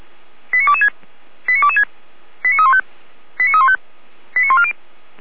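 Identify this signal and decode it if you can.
Five-tone selective call (ZVEI), heard as quick runs of five tones, tones of about 70 ms. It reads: 89226 89226 89315 89315 89140
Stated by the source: ZVEI five-tone selective call received via SDR.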